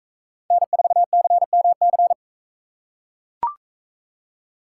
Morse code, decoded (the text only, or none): N4CMC